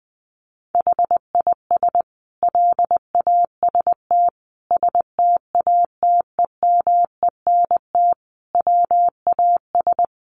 Morse code, heard as a keyed tone, about 20 wpm, 700 Hz. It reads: HIS LAST STATEMENT WAS
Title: Sentences from Top 1200 Words 20wpm